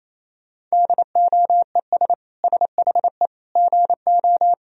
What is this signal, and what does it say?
Morse code, heard as a keyed tone, about 28 wpm, 700 Hz. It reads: DOES SHE GO
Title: Sentences from Top 1000 Words - Review Part 2 - 28wpm